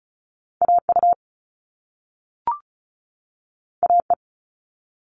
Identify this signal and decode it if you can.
Morse code, heard as a keyed tone, about 36 wpm, 690 Hz. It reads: AU AE
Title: All Cut Numbers - 2-Digit 35wpm